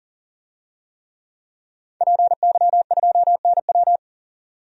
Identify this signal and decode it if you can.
Morse code, heard as a keyed tone, about 40 wpm, 700 Hz. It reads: PY2NW
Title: DXCC-100 Call Signs - 5 Characters - 40wpm